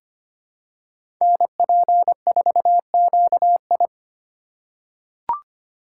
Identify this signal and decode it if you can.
Morse code, heard as a keyed tone, about 25 wpm, 700 Hz. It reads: NP4QI
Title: USA Call Signs - 5 Characters - 2x Character Spacing - 25wpm